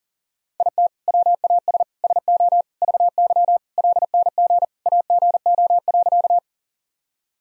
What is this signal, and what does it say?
Morse code, heard as a keyed tone, about 40 wpm, 700 Hz. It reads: IT WAS SO VY LNG AGO.
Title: Sentences from Top 500 Words and Abbreviations 40wpm